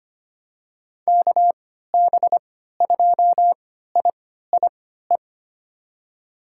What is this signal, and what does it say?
Morse code, heard as a keyed tone, about 25 wpm, spaced as for 14 wpm, 700 Hz. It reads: KB2IIE
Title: USA Call Signs - 6 Characters - 3x Character Spacing - 25wpm